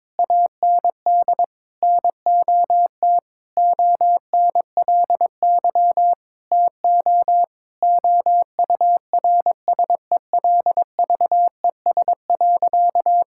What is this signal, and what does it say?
Morse code, heard as a keyed tone, about 22 wpm, 700 Hz. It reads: AND NOT ONLY TO OURSELVES.
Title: Sentences from Top 1600 Words 22wpm